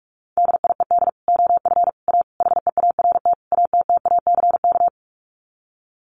Morse code, heard as a keyed tone, about 44 wpm, 700 Hz. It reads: DIED OF A HEART ATTACK
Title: Sets of 5 Words Encore2 45wpm